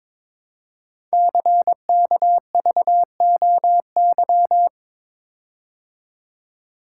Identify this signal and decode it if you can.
Morse code, heard as a keyed tone, about 22 wpm, 700 Hz. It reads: CKVOY
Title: Code Groups - 5 Letters - 22wpm